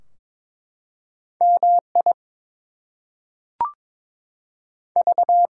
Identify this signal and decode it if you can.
Morse code, heard as a keyed tone, about 22 wpm, spaced as for 5 wpm, 700 Hz. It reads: MIV
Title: US States and Canadian Provinces 22wpm